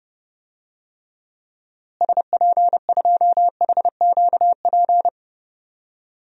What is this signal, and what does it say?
Morse code, heard as a keyed tone, about 30 wpm, 700 Hz. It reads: SP2HQP
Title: DXCC-100 Call Signs - 6 Characters - Repeated 3x - 30wpm